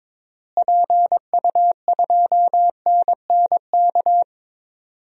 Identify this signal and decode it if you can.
Morse code, heard as a keyed tone, about 22 wpm, 700 Hz. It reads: PU2NNK
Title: DXCC-100 Call Signs - 6 Characters - Repeated 2x - 22wpm